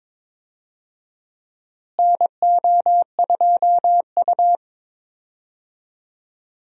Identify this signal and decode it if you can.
Morse code, heard as a keyed tone, about 22 wpm, 700 Hz. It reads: NO2U
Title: USA Call Signs - 4 Characters - Repeated 3x - 22wpm